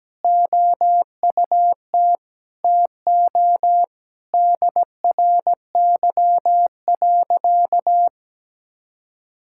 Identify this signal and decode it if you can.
Morse code, heard as a keyed tone, about 17 wpm, 700 Hz. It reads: OUT TO DRY.
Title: Sentences from Top 1600 Words 17wpm